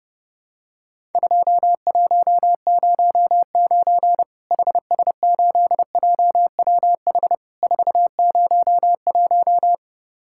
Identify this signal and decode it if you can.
Morse code, heard as a keyed tone, about 30 wpm, 700 Hz. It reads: 2109 HS8JWH 401
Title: Worked All Europe DX Contest - QTC Exchange - 30wpm